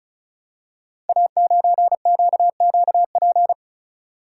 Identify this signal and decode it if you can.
Morse code, heard as a keyed tone, about 36 wpm, 700 Hz. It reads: A9QQP
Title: Code Groups - 5 Letter:Numbers - 35wpm